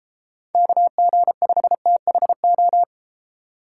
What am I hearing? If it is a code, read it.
Morse code, heard as a keyed tone, about 32 wpm, 700 Hz. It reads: KG5THO